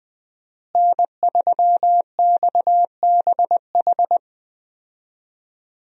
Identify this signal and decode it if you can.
Morse code, heard as a keyed tone, about 20 wpm, 700 Hz. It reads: N3XBH
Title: USA Call Signs - 5 Characters - Repeated 2x - 20wpm